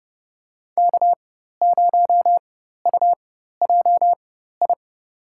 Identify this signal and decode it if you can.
Morse code, heard as a keyed tone, about 30 wpm, spaced as for 14 wpm, 700 Hz. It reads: K0UJI
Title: USA Call Signs - 5 Characters - 4x Character Spacing - 30wpm